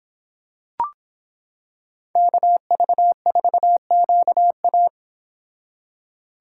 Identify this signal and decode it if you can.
Morse code, heard as a keyed tone, about 26 wpm, 700 Hz. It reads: KV4QA